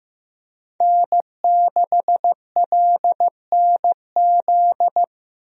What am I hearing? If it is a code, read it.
Morse code, heard as a keyed tone, about 15 wpm, 700 Hz. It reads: N6LNZ
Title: USA Call Signs - 5 Characters - Repeated 2x - 15wpm